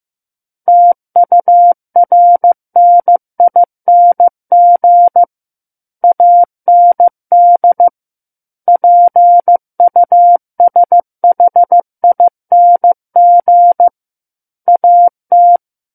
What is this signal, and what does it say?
Morse code, heard as a keyed tone, about 15 wpm, 700 Hz. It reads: TURNING AND PUSHING AT
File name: Book One / 06-Chapter 6 - 15 WPM